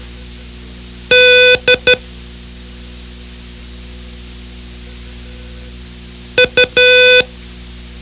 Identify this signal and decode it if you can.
Morse code, heard as a keyed tone, about 12 wpm, 500 Hz. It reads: D U
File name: ham radio / CodeQuick / Code Quick 04